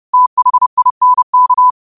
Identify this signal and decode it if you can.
Morse code, heard as a keyed tone, about 30 wpm, 1000 Hz. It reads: THINK
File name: qotdfiles / 30wpm